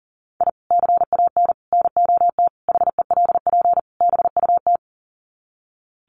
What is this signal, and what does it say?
Morse code, heard as a keyed tone, about 40 wpm, 700 Hz. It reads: I CAN NOT HELP BUT
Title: Sets of 5 Words Encore2 40wpm